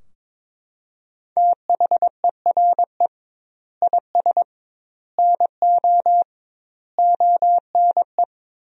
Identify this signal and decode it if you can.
Morse code, heard as a keyed tone, about 21 wpm, 700 Hz. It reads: THERE IS NO ONE
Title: 2x Word-Spacing - Sentences from Top 200 Words - 22wpm